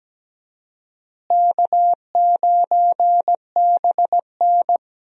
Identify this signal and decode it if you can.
Morse code, heard as a keyed tone, about 17 wpm, 700 Hz. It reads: K9BN
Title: USA Call Signs - 4 Characters - Repeated 2x - 17wpm